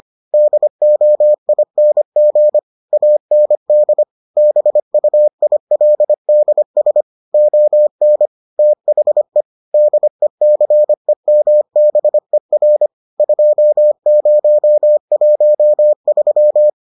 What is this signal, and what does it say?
Morse code, heard as a keyed tone, about 25 wpm, 600 Hz. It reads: DOING AND BUILDS ON THE DECEMBER 2013